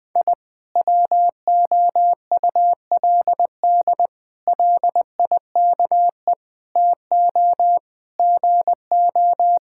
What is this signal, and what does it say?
Morse code, heard as a keyed tone, about 20 wpm, 700 Hz. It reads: I WOULD LIKE TO GO
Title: Sentences with 3-Word Phrases - 20wpm